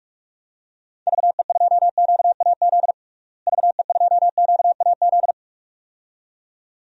Morse code, heard as a keyed tone, about 46 wpm, 700 Hz. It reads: VE2QAZ VE2QAZ
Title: DXCC-100 Call Signs - 6 Characters - Repeated 2x - 45wpm